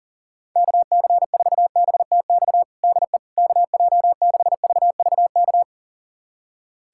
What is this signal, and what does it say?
Morse code, heard as a keyed tone, about 40 wpm, 700 Hz. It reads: KC4BTX DE KJ6VVK
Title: USA Callsign DE Callsign - 6 Characters - 40wpm